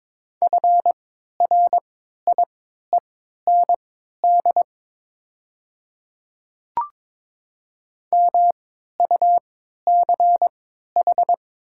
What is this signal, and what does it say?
Morse code, heard as a keyed tone, about 22 wpm, spaced as for 12 wpm, 700 Hz. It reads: FRIEND MUCH
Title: 3x Character Spacing - Top 500 Words - 22wpm